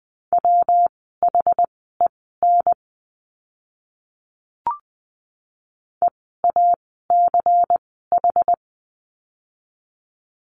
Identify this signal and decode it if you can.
Morse code, heard as a keyed tone, about 20 wpm, spaced as for 14 wpm, 700 Hz. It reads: WHEN EACH
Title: 192 Lesson 31f - Letter v - Words 20wpm